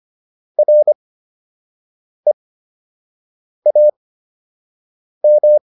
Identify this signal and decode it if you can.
Morse code, heard as a keyed tone, about 25 wpm, spaced as for 6 wpm, 600 Hz. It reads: REAM